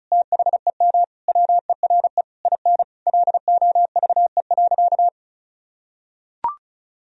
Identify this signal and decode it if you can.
Morse code, heard as a keyed tone, about 36 wpm, 700 Hz. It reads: THEM WERE IN LOVE.
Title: Sentences from Top 1000 Words - Review Part 4 - 35wpm